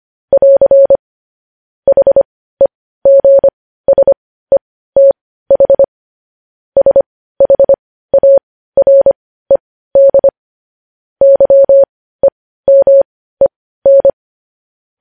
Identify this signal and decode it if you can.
Morse code, heard as a keyed tone, about 25 wpm, spaced as for 15 wpm, 550 Hz. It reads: <AR> HEGSETH SHARED YEMEN
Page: News headlines in morse code at 15 wpm